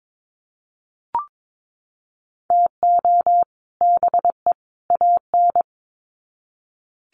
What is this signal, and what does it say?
Morse code, heard as a keyed tone, about 22 wpm, 700 Hz. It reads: TO BE AN
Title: Sets of 3 Words 22wpm